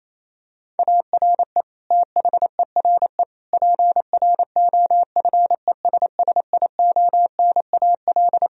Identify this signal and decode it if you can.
Morse code, heard as a keyed tone, about 27 wpm, 700 Hz. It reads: ARE THERE PROFESSIONAL
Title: Sentences from Top 1000 Words - Review Part 4 - 28wpm